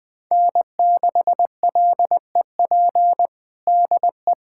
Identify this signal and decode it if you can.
Morse code, heard as a keyed tone, about 21 wpm, 700 Hz. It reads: N6LEP DE
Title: USA Callsign DE Callsign - 5 Characters - 20wpm